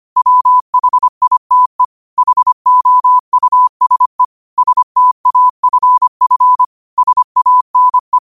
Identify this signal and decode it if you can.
Morse code, heard as a keyed tone, about 25 wpm, 1000 Hz. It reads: WHITE HOUSE STAFF SANE
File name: qotdfiles / 25wpm